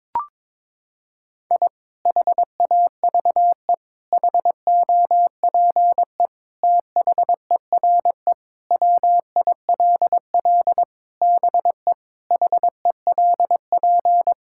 Morse code, heard as a keyed tone, about 22 wpm, 700 Hz. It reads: I HAVE HOPE THERE WILL BE HELP
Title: Sentences from Top 1000 Words - Review Part 5 - 22wpm